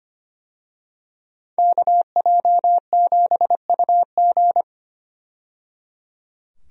Morse code, heard as a keyed tone, about 25 wpm, 700 Hz. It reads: KJ7UG
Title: USA Call Signs - 5 Characters - 25wpm